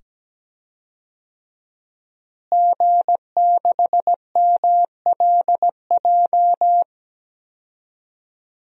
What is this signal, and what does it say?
Morse code, heard as a keyed tone, about 17 wpm, 700 Hz. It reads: G6MLJ
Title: DXCC-100 Call Signs - 5 Characters - Repeated 2x - 17wpm